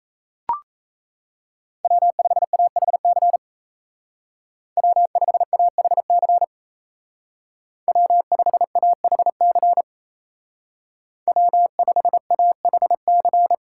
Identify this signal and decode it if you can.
Morse code, heard as a keyed tone, about 33 wpm, 700 Hz. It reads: W5AHC W5AHC W5AHC W5AHC
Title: Speed Racing - USA 1x3 Call Signs 28wpm